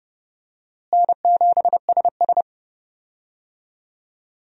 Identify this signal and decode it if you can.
Morse code, heard as a keyed tone, about 30 wpm, 700 Hz. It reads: N7SS
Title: USA Call Signs - 4 Characters - Repeated 2x - 30wpm